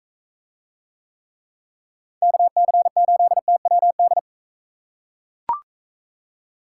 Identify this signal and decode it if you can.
Morse code, heard as a keyed tone, about 43 wpm, 700 Hz. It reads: KC8TWD